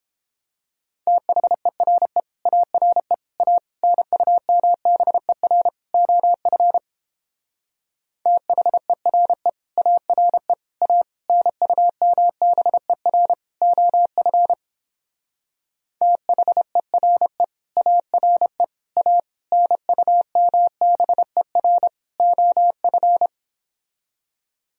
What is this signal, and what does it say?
Morse code, heard as a keyed tone, about 29 wpm, 700 Hz. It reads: THERE ARE A NUMBER OF THERE ARE A NUMBER OF THERE ARE A NUMBER OF